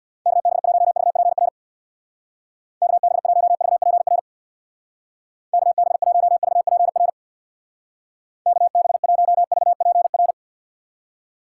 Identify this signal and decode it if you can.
Morse code, heard as a keyed tone, about 62 wpm, 700 Hz. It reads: KB1VPR KB1VPR KB1VPR KB1VPR
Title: Speed Racing - USA 2x3 Call Signs 50wpm